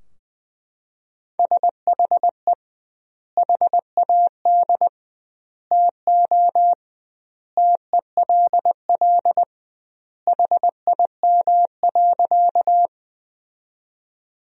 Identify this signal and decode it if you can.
Morse code, heard as a keyed tone, about 20 wpm, 700 Hz. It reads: SHE HAD TO TELL HIM.
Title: 2x Word-Spacing - Sentences from Top 200 Words - 20wpm